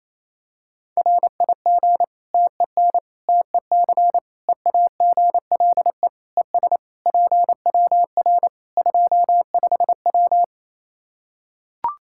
Morse code, heard as a keyed tone, about 28 wpm, 700 Hz. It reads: RIG TEN TEC EAGLE ES PWR 25W
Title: Practice QSO - Exchange 2 - Info - 28wpm